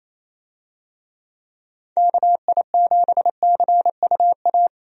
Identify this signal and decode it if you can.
Morse code, heard as a keyed tone, about 27 wpm, 700 Hz. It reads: KI7CUA
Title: USA Call Signs - 6 Characters - Repeated 2x - 28wpm